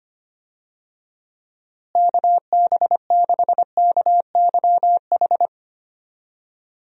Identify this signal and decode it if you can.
Morse code, heard as a keyed tone, about 25 wpm, 700 Hz. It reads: KB6KYH
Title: USA Call Signs - 6 Characters - Repeated 2x - 25wpm